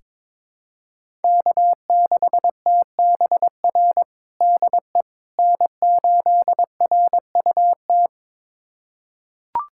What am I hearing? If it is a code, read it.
Morse code, heard as a keyed tone, about 22 wpm, 700 Hz. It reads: K6TBR DE N8RUT
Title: USA Callsign DE Callsign - 5 Characters - 22wpm